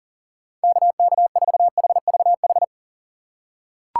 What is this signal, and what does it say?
Morse code, heard as a keyed tone, about 37 wpm, 700 Hz. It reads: KK4HVH
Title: USA Call Signs - 6 Characters - 6x Character Spacing - 40wpm